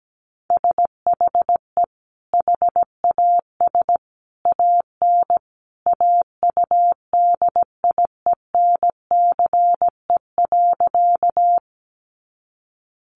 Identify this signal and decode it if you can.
Morse code, heard as a keyed tone, about 17 wpm, 700 Hz. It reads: SHE HAS AN AUDIENCE.
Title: Sentences from Top 800 Words 17wpm